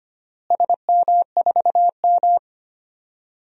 Morse code, heard as a keyed tone, about 25 wpm, 700 Hz. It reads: SM4M
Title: DXCC-100 Call Signs - 4 Characters - Repeated 2x - 25wpm